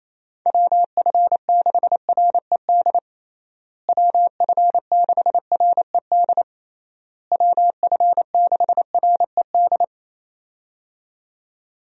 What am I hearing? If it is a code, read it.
Morse code, heard as a keyed tone, about 28 wpm, 700 Hz. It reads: WF6RED WF6RED WF6RED
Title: USA Call Signs - 6 Characters - Repeated 3x - 28wpm